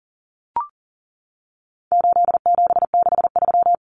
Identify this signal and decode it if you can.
Morse code, heard as a keyed tone, about 40 wpm, 700 Hz. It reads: 8763
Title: Common Cut Numbers - 4-Digit 40wpm